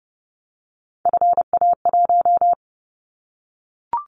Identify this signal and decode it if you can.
Morse code, heard as a keyed tone, about 30 wpm, 700 Hz. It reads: FA1
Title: Callsign Prefixes for Top 50 DXCC - 30wpm